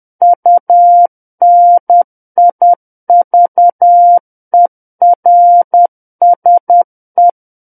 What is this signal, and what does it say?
Morse code, heard as a keyed tone, about 10 wpm, 700 Hz. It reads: UNIVERSE
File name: Book One / 01-Chapter 1 - 10 WPM